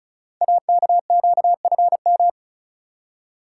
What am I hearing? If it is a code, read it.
Morse code, heard as a keyed tone, about 35 wpm, 690 Hz. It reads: AKQFM
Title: Code Groups - 5 Letter:Numbers - 35wpm